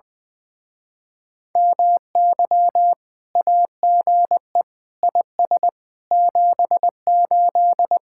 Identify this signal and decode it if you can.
Morse code, heard as a keyed tone, about 20 wpm, 700 Hz. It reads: MY AGE IS 78
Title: Speed-Racing Common QSO Phrases - 20wpm